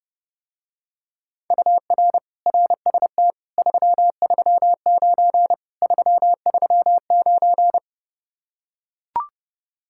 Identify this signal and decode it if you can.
Morse code, heard as a keyed tone, about 30 wpm, 700 Hz. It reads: UR RST 339 339